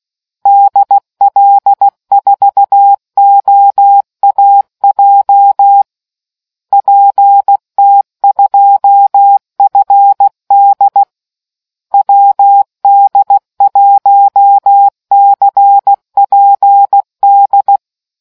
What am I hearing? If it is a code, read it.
Morse code, heard as a keyed tone, about 16 wpm, 800 Hz. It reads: DL4OAJ PT2FD WD1CPD